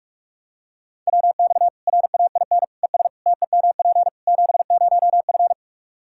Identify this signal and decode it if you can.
Morse code, heard as a keyed tone, about 46 wpm, 700 Hz. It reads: WX RAIN ES TEMP 70F
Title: Practice QSO - Exchange 2 - Info - 45wpm